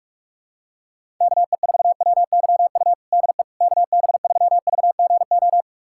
Morse code, heard as a keyed tone, about 46 wpm, 700 Hz. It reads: KE4WYU DE KB3VGO
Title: USA Callsign DE Callsign - 6 Characters - 45wpm